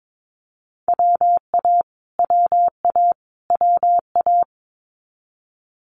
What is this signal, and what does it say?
Morse code, heard as a keyed tone, about 22 wpm, 700 Hz. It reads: WA WA WA
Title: US States Repeated 3x 22wpm